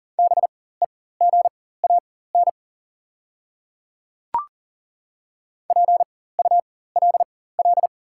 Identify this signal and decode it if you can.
Morse code, heard as a keyed tone, about 39 wpm, spaced as for 19 wpm, 690 Hz. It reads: BEGAN PULL
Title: 4x Character Spacing - Top 500 Words - 40wpm